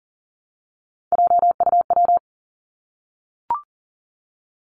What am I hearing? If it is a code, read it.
Morse code, heard as a keyed tone, about 40 wpm, 700 Hz. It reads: JUW
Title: Three Challenging Letter-Number Combinations 40wpm